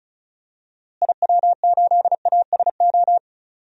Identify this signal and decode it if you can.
Morse code, heard as a keyed tone, about 34 wpm, 700 Hz. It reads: IW8ASO
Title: DXCC-100 Call Signs - 6 Characters - Repeated 2x - 35wpm